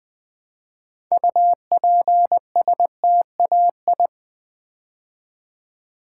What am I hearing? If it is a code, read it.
Morse code, heard as a keyed tone, about 20 wpm, 700 Hz. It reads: UPSTAI
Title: Build A Word - 2 Letters At A Time - 8-Letter Words - 20wpm